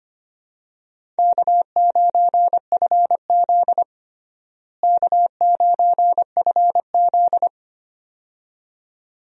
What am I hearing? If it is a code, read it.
Morse code, heard as a keyed tone, about 25 wpm, 700 Hz. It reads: K9FZ K9FZ
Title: USA Call Signs - 4 Characters - Repeated 2x - 25wpm